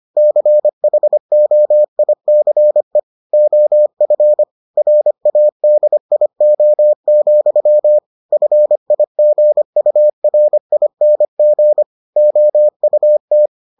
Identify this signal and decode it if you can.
Morse code, heard as a keyed tone, about 25 wpm, 600 Hz. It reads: CHOICE OF RADIO, FIGURING OUT